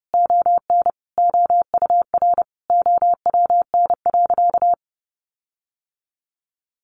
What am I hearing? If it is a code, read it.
Morse code, heard as a keyed tone, about 30 wpm, 700 Hz. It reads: ON OUR OWN.
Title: Sentences from Top 1500 Words 30wpm